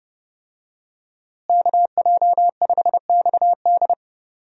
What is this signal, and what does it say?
Morse code, heard as a keyed tone, about 30 wpm, 700 Hz. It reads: KJ5XD
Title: USA Call Signs - 5 Characters - Repeated 2x - 30wpm